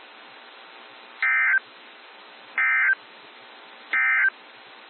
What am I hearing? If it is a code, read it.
Emergency Alert System (SAME) tones, heard as harsh buzzing bursts. It is heard from 1.1 to 4.4 s.
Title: NOAA Weather Radio - WXJ73